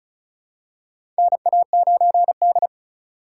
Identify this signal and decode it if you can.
Morse code, heard as a keyed tone, about 36 wpm, 700 Hz. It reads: NA9D
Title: USA Call Signs - 4 Characters - 8x Character Spacing - 35wpm